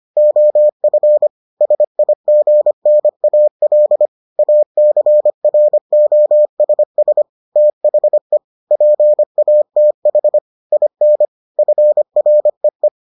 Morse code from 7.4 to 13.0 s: THE PATH IN FREE

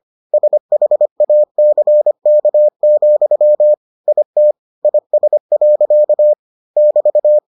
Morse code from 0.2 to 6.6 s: SHACK, IT IS.

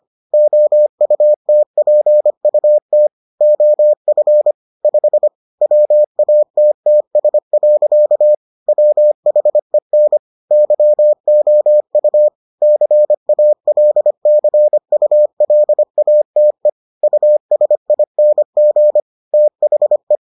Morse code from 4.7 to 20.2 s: 5 WATTS. WHEN YOU CALCULATE USING THE

